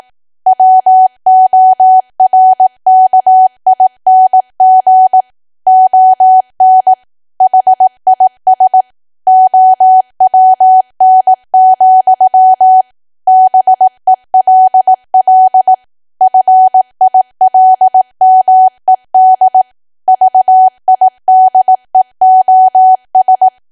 Morse code, heard as a keyed tone, about 18 wpm, 750 Hz. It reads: WORKING ON HIS OWN, BELL FILMED VIDEOS
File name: Archive / 18 WPM / 18WPM